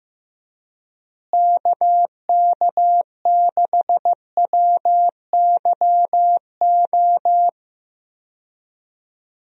Morse code, heard as a keyed tone, about 15 wpm, 700 Hz. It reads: KK6WYO
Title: USA Call Signs - 6 Characters - Repeated 2x - 15wpm